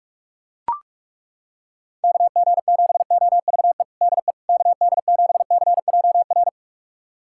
Morse code, heard as a keyed tone, about 41 wpm, 700 Hz. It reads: KG7OVE DE KD7CJR
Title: USA Callsign DE Callsign 45wpm